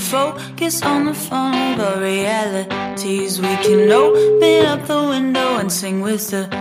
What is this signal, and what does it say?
Ringback tone (Europe, 425 Hz), with noise.